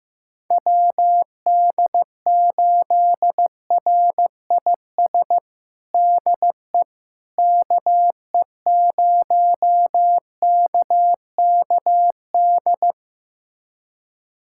Morse code, heard as a keyed tone, about 15 wpm, 700 Hz. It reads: WD8RIS DE KE0KKD
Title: USA Callsign DE Callsign - 6 Characters - 15wpm